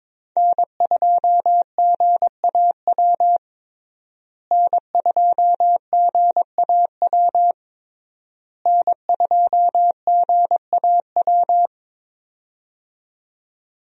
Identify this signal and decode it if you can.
Morse code, heard as a keyed tone, about 22 wpm, 700 Hz. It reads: N2GAW N2GAW N2GAW
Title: USA Call Signs - 5 Characters - Repeated 3x - 22wpm